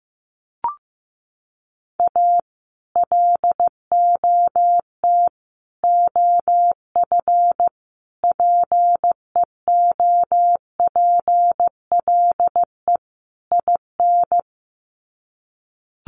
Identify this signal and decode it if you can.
Morse code, heard as a keyed tone, about 15 wpm, 700 Hz. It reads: A LOT OF PEOPLE IN
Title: Sets of 5 Words 15wpm